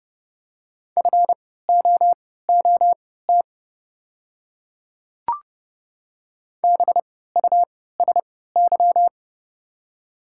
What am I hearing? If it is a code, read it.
Morse code, heard as a keyed tone, about 30 wpm, spaced as for 17 wpm, 690 Hz. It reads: FOOT BUSY
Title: 3x Character Spacing - Top 500 Words - 30wpm